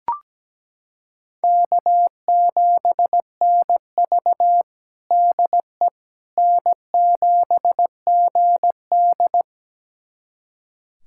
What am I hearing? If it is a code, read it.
Morse code, heard as a keyed tone, about 17 wpm, 700 Hz. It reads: K7NV DE N7GD
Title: USA Callsign DE Callsign - 4 Characters - 17wpm